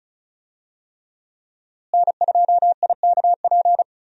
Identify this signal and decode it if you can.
Morse code, heard as a keyed tone, about 34 wpm, 700 Hz. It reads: N2IKP
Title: USA Call Signs - 5 Characters - Repeated 3x - 35wpm